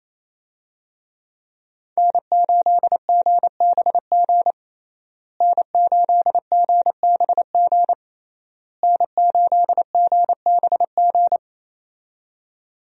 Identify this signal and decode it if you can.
Morse code, heard as a keyed tone, about 28 wpm, 700 Hz. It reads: N8GBG N8GBG N8GBG
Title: USA Call Signs - 5 Characters - Repeated 3x - 28wpm